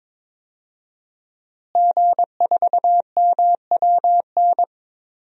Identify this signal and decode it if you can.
Morse code, heard as a keyed tone, about 21 wpm, 700 Hz. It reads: G4MWN
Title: DXCC-100 Call Signs - 5 Characters - Repeated 3x - 22wpm